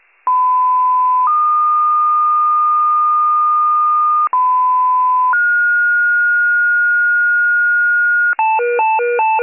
Two-tone page, heard as a one-second tone, then a three-second tone, about 1010 then 1220 Hz, about 1010 then 1430 Hz, with noise. It is heard from 0.3 to 8.3 s.